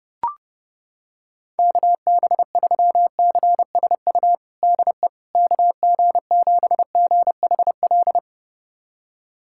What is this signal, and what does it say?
Morse code, heard as a keyed tone, about 29 wpm, 700 Hz. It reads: KB3CSU DE KG7GHL
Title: USA Callsign DE Callsign - 6 Characters - 30wpm